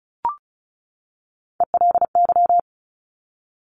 Morse code, heard as a keyed tone, about 36 wpm, 700 Hz. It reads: ELY
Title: Top Three Letter Combinations in Words 35wpm